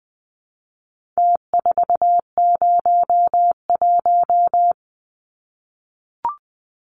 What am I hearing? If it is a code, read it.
Morse code, heard as a keyed tone, about 20 wpm, 700 Hz. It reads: T401
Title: Numbers - Contest Style - 4-Digit 20wpm